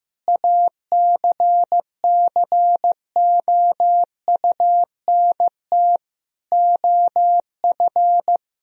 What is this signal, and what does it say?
Morse code, heard as a keyed tone, about 15 wpm, 700 Hz. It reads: ACCOUNT OF
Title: Sentences from Top 1000 Words - Review Part 3 - 15wpm